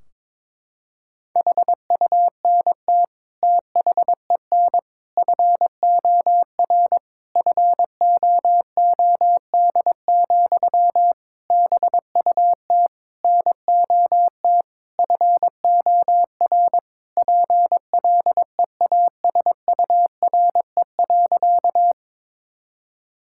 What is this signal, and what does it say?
Morse code, heard as a keyed tone, about 22 wpm, 700 Hz. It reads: HUNT THEN FOR FOOD, BUT NOT FOR PLEASURE.
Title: Sentences from Top 1600 Words 22wpm